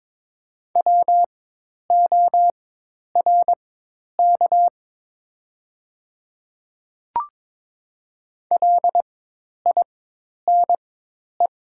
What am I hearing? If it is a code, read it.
Morse code, heard as a keyed tone, about 22 wpm, spaced as for 10 wpm, 700 Hz. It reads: WORK LINE
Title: 4x Character Spacing - Top 200 Words - 22wpm